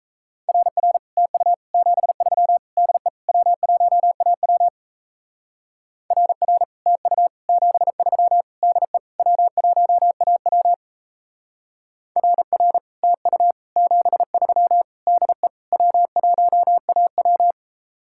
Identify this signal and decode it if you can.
Morse code, heard as a keyed tone, about 34 wpm, 700 Hz. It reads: RR TU 73 DE W1AW RR TU 73 DE W1AW RR TU 73 DE W1AW
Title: Speed-Racing Common QSO Phrases - 28wpm